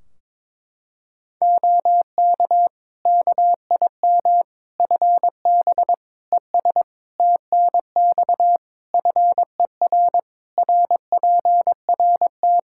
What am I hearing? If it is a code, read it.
Morse code, heard as a keyed tone, about 22 wpm, 700 Hz. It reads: OK KIM FB ES TNX FER RPRT
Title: Practice QSO - Exchange 2 - Info - 22wpm